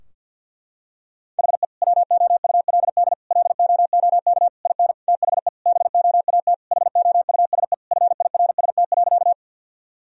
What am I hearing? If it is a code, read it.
Morse code, heard as a keyed tone, about 52 wpm, 700 Hz. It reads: HE WOULD LOOK IN THE BOAT HOUSE FIRST.
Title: Sentences from Top 1000 Words - Review Part 3 - 50wpm